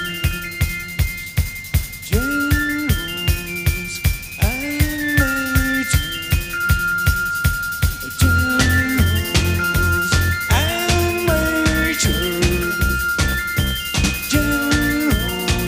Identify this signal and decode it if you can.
Dial-up modem handshake.